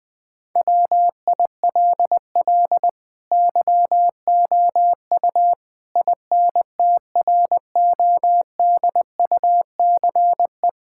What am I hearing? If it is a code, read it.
Morse code, heard as a keyed tone, about 20 wpm, 700 Hz. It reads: WILL YOU INTRODUCE
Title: Sentences from Top 1600 Words 20wpm